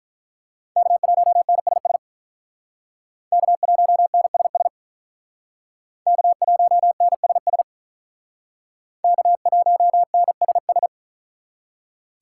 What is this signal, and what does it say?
Morse code, heard as a keyed tone, about 43 wpm, 700 Hz. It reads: K1NSS K1NSS K1NSS K1NSS
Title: Speed Racing - USA 1x3 Call Signs 35wpm